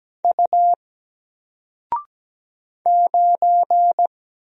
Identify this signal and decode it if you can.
Morse code, heard as a keyed tone, about 17 wpm, 700 Hz. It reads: U 9
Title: Single Character (Letter:Number:Punctuation:Prosign) 17wpm